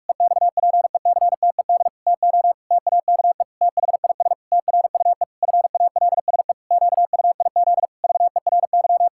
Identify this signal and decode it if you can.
Morse code, heard as a keyed tone, about 42 wpm, 700 Hz. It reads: EXPECTED TO TAKE THIS TRUE FALSE QUIZ VERY
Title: Pangram Challenge v2 - 45wpm